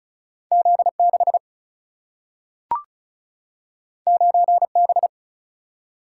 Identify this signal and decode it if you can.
Morse code, heard as a keyed tone, about 36 wpm, 700 Hz. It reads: Z6 9B
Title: Single-Letter Single-Number 35wpm